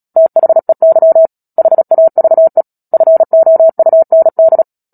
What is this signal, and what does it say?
Morse code, heard as a keyed tone, about 37 wpm, 650 Hz. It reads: THEY HAVE FOUND
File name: Book Two / 10-Chapter 10 - Epilogue - 36 WPM